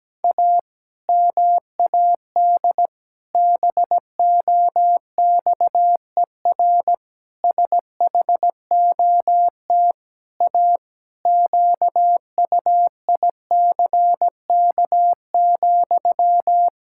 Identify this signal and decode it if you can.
Morse code, heard as a keyed tone, about 17 wpm, 700 Hz. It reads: A MAD BOXER SHOT A QUICK,
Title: Pangram Challenge v2 - 17wpm